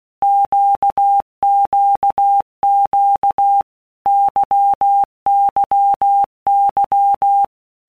Morse code, heard as a keyed tone, about 16 wpm, 800 Hz. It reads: QQQ YYY
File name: Basic Letter Groups By Blocks / Groups@15 / LFCYQ@15